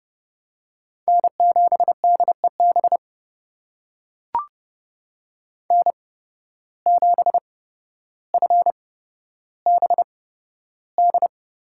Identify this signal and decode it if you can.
Morse code, heard as a keyed tone, about 30 wpm, spaced as for 13 wpm, 700 Hz. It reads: N7DEB N7FBD
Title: USA Call Signs - 5 Characters - 8x Character Spacing - 30wpm